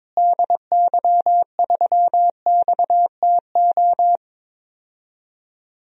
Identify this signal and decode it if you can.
Morse code, heard as a keyed tone, about 22 wpm, 700 Hz. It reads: DY3XTO
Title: DXCC-100 Call Signs - 6 Characters - Repeated 3x - 22wpm